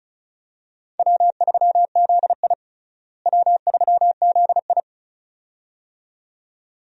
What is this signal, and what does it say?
Morse code, heard as a keyed tone, about 34 wpm, 700 Hz. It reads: W3ZI W3ZI
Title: USA Call Signs - 4 Characters - Repeated 2x - 35wpm